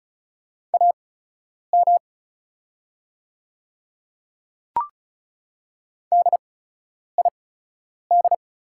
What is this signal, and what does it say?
Morse code, heard as a keyed tone, about 34 wpm, spaced as for 9 wpm, 690 Hz. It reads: AM DID
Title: 8x Character Spacing - Top 500 Words - 35wpm